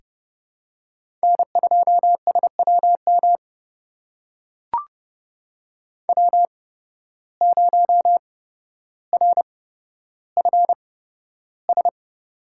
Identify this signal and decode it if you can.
Morse code, heard as a keyed tone, about 31 wpm, spaced as for 13 wpm, 700 Hz. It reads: N2SWM W0RFS
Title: USA Call Signs - 5 Characters - 8x Character Spacing - 30wpm